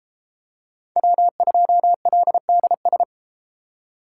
Morse code, heard as a keyed tone, about 31 wpm, 700 Hz. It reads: W2LDS